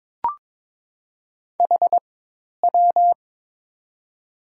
Morse code, heard as a keyed tone, about 22 wpm, 700 Hz. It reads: H W
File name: Top 90 QSO Elements - 4x Character Spacing - 22wpm